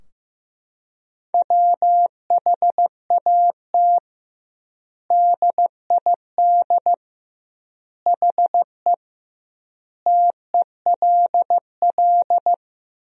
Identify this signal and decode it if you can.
Morse code, heard as a keyed tone, about 16 wpm, 700 Hz. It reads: WHAT DID HE TELL
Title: 2x Word-Spacing - Sentences from Top 200 Words - 15wpm